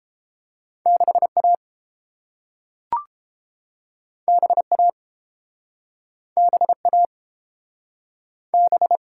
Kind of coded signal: Morse code

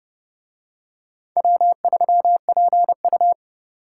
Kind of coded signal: Morse code

